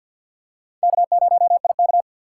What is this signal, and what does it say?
Morse code, heard as a keyed tone, about 49 wpm, 700 Hz. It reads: K0IK
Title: USA Call Signs - 4 Characters - 3x Character Spacing - 50wpm